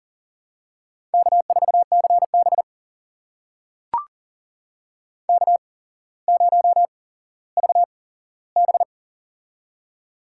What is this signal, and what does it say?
Morse code, heard as a keyed tone, about 40 wpm, spaced as for 17 wpm, 700 Hz. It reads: K4CB K0VB
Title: USA Call Signs - 4 Characters - 8x Character Spacing - 40wpm